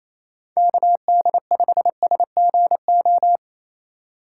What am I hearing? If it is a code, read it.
Morse code, heard as a keyed tone, about 28 wpm, 700 Hz. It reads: KD5SGO
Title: USA Call Signs - 6 Characters - 3x Character Spacing - 28wpm